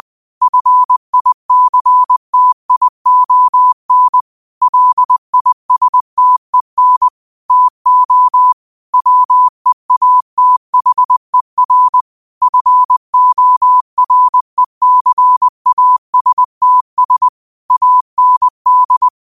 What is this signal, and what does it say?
Morse code, heard as a keyed tone, about 20 wpm, 1000 Hz. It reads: FICTION LISTEN TO WEATHER FORECASTS AND
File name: qotdfiles / 20wpm